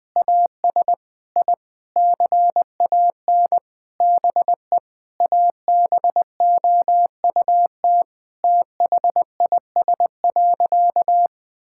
Morse code, heard as a keyed tone, about 20 wpm, 700 Hz. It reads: AS I CAN BE ABOUT THIS.
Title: Sentences with 9-Word Phrases - 20wpm